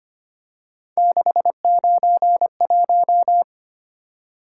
Morse code, heard as a keyed tone, about 25 wpm, 700 Hz. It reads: 691
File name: Three Letter-Number Combinations - 25wpm